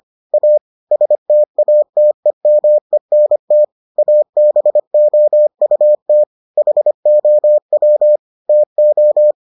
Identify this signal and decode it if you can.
Morse code, heard as a keyed tone, about 25 wpm, 600 Hz. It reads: A STATEMENT ABOUT HOW TO